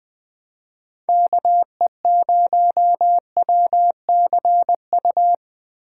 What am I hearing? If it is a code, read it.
Morse code, heard as a keyed tone, about 20 wpm, 700 Hz. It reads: KE0WCU
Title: Speed Racing - USA 2x3 Call Signs 17wpm